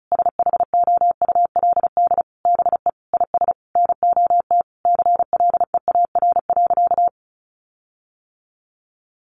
Morse code, heard as a keyed tone, about 34 wpm, 700 Hz. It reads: SHOULD BE IS NOT CLEAR.
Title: Sentences from Top 1300 Words 35wpm